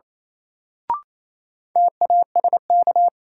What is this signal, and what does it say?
Morse code, heard as a keyed tone, about 27 wpm, 700 Hz. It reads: TASK
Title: Sentences from Top 1000 Words - Review Part 7 - New Word Drill - 28wpm